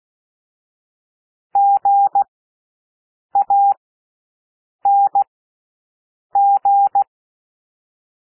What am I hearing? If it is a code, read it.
Morse code, heard as a keyed tone, about 16 wpm, spaced as for 6 wpm, 800 Hz. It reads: GANG
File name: cwmp3 / W1 SOUND WORDS 16 wpm